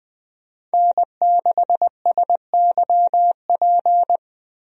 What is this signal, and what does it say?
Morse code, heard as a keyed tone, about 20 wpm, 700 Hz. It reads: N6SYP
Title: USA Call Signs - 5 Characters - 20wpm